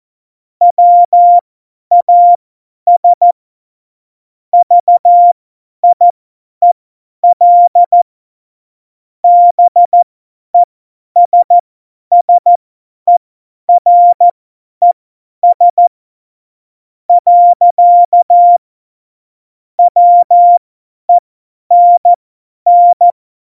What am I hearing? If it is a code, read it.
Morse code, heard as a keyed tone, about 14 wpm, spaced as for 10 wpm, 700 Hz. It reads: WAS VIEL BESSERES. WENN
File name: Morsekurs / Witze / Witz 25. Dezember, 50 F70